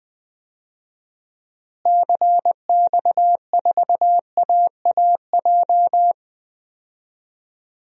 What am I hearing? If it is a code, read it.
Morse code, heard as a keyed tone, about 20 wpm, 700 Hz. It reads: CX4AAJ
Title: DXCC-100 Call Signs - 6 Characters - Repeated 3x - 20wpm